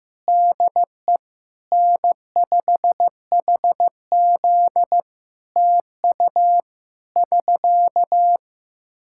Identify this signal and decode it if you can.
Morse code, heard as a keyed tone, about 15 wpm, 700 Hz. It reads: DE N5HZ TU <SK>
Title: Practice QSO - Exchange 4 - Ending - 15wpm